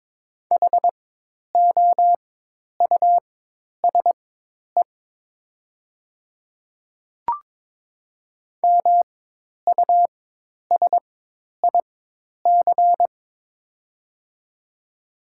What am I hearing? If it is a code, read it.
Morse code, heard as a keyed tone, about 22 wpm, spaced as for 10 wpm, 700 Hz. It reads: HOUSE MUSIC
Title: 4x Character Spacing - Top 300 Words - 22wpm